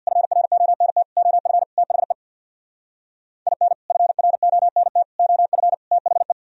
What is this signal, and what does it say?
Morse code, heard as a keyed tone, about 64 wpm, 700 Hz. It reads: FRONT OF THE IN FRONT OF THE